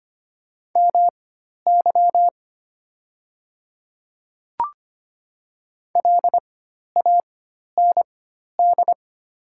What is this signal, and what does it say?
Morse code, heard as a keyed tone, about 25 wpm, spaced as for 12 wpm, 700 Hz. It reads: MY LAND
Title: 4x Character Spacing - Top 200 Words - 25wpm